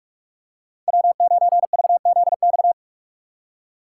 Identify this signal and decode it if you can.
Morse code, heard as a keyed tone, about 43 wpm, 700 Hz. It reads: W9VZX
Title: USA Call Signs - 5 Characters - 3x Character Spacing - 45wpm